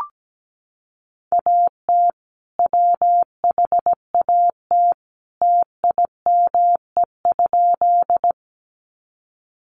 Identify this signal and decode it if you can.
Morse code, heard as a keyed tone, about 17 wpm, 700 Hz. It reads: AT WHAT TIME?
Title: Sentences from Top 100 Words 17wpm